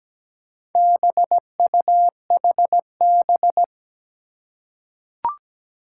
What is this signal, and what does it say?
Morse code, heard as a keyed tone, about 17 wpm, 700 Hz. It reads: BUHB
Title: Four Challenging Letter-Number Combinations 17wpm